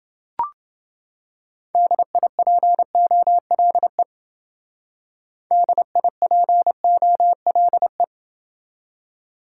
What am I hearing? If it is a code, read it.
Morse code, heard as a keyed tone, about 28 wpm, 700 Hz. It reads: DIPOLE DIPOLE